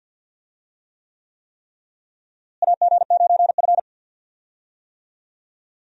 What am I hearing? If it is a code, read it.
Morse code, heard as a keyed tone, about 50 wpm, 700 Hz. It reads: AG9F
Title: USA Call Signs - 4 Characters - Repeated 3x - 50wpm